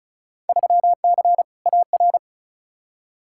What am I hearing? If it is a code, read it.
Morse code, heard as a keyed tone, about 34 wpm, 700 Hz. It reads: 3C AR
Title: Field Day Exchanges Encore - 35wpm